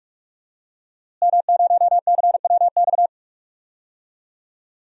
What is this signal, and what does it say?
Morse code, heard as a keyed tone, about 43 wpm, 700 Hz. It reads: M0CWX
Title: DXCC-100 Call Signs - 5 Characters - Repeated 3x - 45wpm